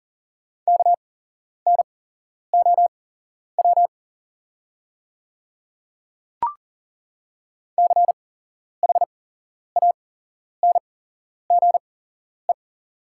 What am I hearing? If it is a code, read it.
Morse code, heard as a keyed tone, about 40 wpm, spaced as for 11 wpm, 700 Hz. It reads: KNOW CHANGE
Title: 8x Character Spacing - Top 200 Words - 40wpm